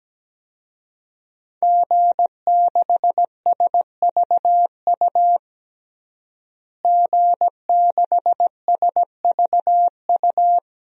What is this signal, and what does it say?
Morse code, heard as a keyed tone, about 17 wpm, 700 Hz. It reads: G6SVU G6SVU
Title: DXCC-100 Call Signs - 5 Characters - Repeated 3x - 17wpm